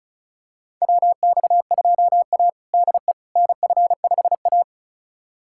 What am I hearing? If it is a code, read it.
Morse code, heard as a keyed tone, about 36 wpm, 700 Hz. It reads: WX2A DE NF5A
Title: USA Callsign DE Callsign - 4 Characters - 35wpm